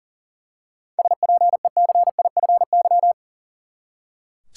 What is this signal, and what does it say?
Morse code, heard as a keyed tone, about 38 wpm, 700 Hz. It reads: SPECIFY
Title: Seven Letter Words - 40wpm